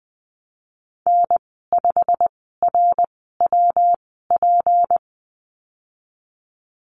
Morse code, heard as a keyed tone, about 19 wpm, spaced as for 14 wpm, 700 Hz. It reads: N5RWP
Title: 173 Lesson 28g - Number 9 - Call Signs 20wpm